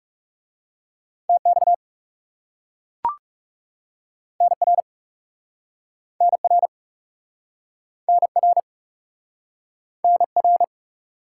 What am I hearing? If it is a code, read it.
Morse code, heard as a keyed tone, about 40 wpm, 700 Hz. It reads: TX NR NR NR NR